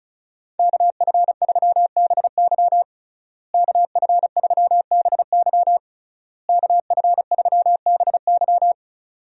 Morse code, heard as a keyed tone, about 36 wpm, 700 Hz. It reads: KF3BY KF3BY KF3BY
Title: USA Call Signs - 5 Characters - Repeated 3x - 35wpm